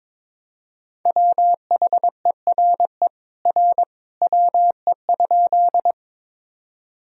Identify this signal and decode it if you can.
Morse code, heard as a keyed tone, about 22 wpm, 700 Hz. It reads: WHERE R WE?
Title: Sentences from Top 200 Words and Abbreviations 22wpm